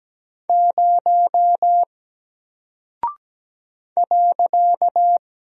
Morse code, heard as a keyed tone, about 17 wpm, 700 Hz. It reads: 0 .
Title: Single Character (Letter:Number:Punctuation:Prosign) Alt 17wpm